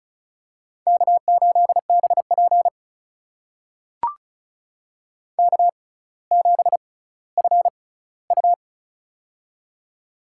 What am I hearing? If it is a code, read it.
Morse code, heard as a keyed tone, about 34 wpm, spaced as for 18 wpm, 700 Hz. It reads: K8BP K7FU
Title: USA Call Signs - 4 Characters - 6x Character Spacing - 35wpm